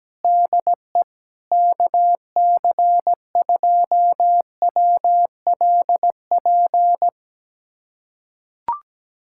Morse code, heard as a keyed tone, about 17 wpm, 700 Hz. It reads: DE KC2WLP
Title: USA Callsign DE Callsign 17wpm